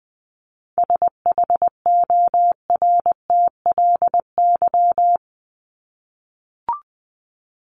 Morse code, heard as a keyed tone, about 20 wpm, 700 Hz. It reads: SHORTLY
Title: LY Suffix - Top 100 Words with Base Words 20wpm